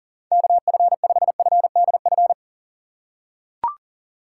Morse code, heard as a keyed tone, about 40 wpm, 700 Hz. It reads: KF5FDF
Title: USA Call Signs - 6 Characters - 3x Character Spacing - 40wpm